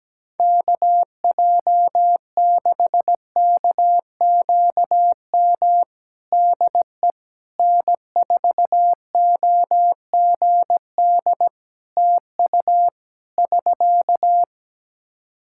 Morse code, heard as a keyed tone, about 17 wpm, 700 Hz. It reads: KJ6KQM DE N4OGD TU <SK>
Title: Practice QSO - Exchange 4 - Ending - 17wpm